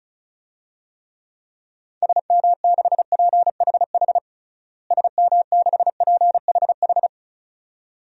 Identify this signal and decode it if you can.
Morse code, heard as a keyed tone, about 36 wpm, 700 Hz. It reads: SM6PHH SM6PHH
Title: DXCC-100 Call Signs - 6 Characters - Repeated 2x - 35wpm